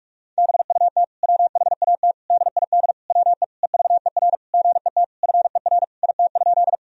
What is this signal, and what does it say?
Morse code, heard as a keyed tone, about 47 wpm, 700 Hz. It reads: BUT WHAT DID WE EVER GET FER IT?
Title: Sentences from Top 400 Words and Abbreviations 45wpm